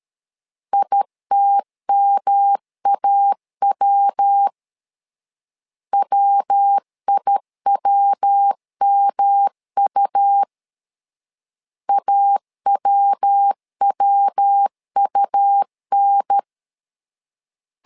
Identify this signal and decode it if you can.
Morse code, heard as a keyed tone, about 13 wpm, 800 Hz. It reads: ITMAW WIWMU AWWUN
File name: LEZIONI / 3 MWRO / 3 MW 12wpm 1633